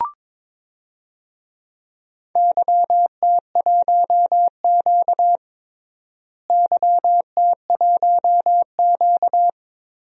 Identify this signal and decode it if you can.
Morse code, heard as a keyed tone, about 22 wpm, 700 Hz. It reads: YT1Q YT1Q
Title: DXCC-100 Call Signs - 4 Characters - Repeated 3x - 22wpm